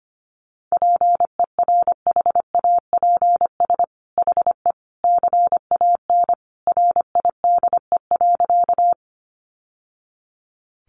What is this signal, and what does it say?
Morse code, heard as a keyed tone, about 25 wpm, 700 Hz. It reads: PERHAPS HE CAN RIDE.
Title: Sentences from Top 1200 Words 25wpm